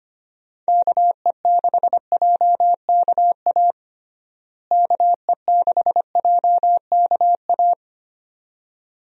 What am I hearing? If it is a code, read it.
Morse code, heard as a keyed tone, about 25 wpm, 700 Hz. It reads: KE6JKA KE6JKA
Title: USA Call Signs - 6 Characters - Repeated 2x - 25wpm